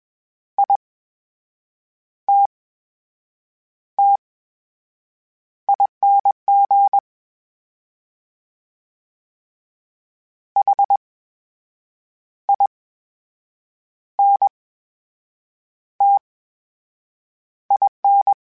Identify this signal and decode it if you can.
Morse code, heard as a keyed tone, about 21 wpm, spaced as for 7 wpm, 800 Hz. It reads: ITTING HINTIN
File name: Special INGs-mp3s / Special INGs-mp3s / VerySlowToFaster Ing @20